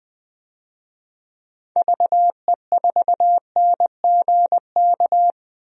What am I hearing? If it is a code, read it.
Morse code, heard as a keyed tone, about 20 wpm, 700 Hz. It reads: VE4NGK
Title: DXCC-100 Call Signs - 6 Characters - 20wpm